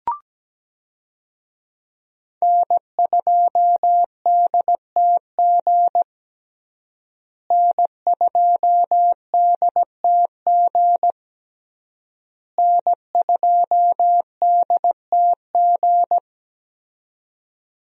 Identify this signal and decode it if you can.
Morse code, heard as a keyed tone, about 17 wpm, 700 Hz. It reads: N2DTG N2DTG N2DTG
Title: USA Call Signs - 5 Characters - Repeated 3x - 17wpm